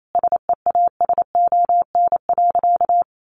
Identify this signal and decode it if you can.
Morse code, heard as a keyed tone, about 28 wpm, 700 Hz. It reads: SEASON.
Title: Sentences from Top 1400 Words 28wpm